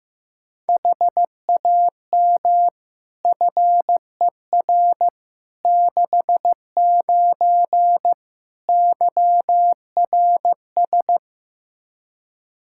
Morse code, heard as a keyed tone, about 15 wpm, 700 Hz. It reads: HAM FER 69 YRS
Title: K1USN SST QSO Practice 15wpm